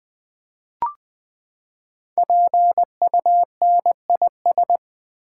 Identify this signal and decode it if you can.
Morse code, heard as a keyed tone, about 20 wpm, 700 Hz. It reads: PUNIS
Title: Build A Word - 5 Letters At A Time - 10-Letter Words - 20wpm